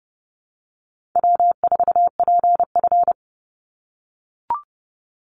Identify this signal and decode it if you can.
Morse code, heard as a keyed tone, about 30 wpm, 700 Hz. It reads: W4PF
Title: Easy Call Signs - 30wpm